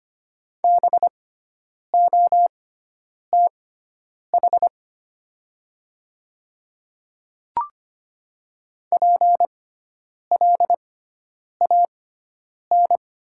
Morse code, heard as a keyed tone, about 25 wpm, spaced as for 9 wpm, 700 Hz. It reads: BOTH PLAN
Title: 6x Character Spacing - Top 500 Words - 25wpm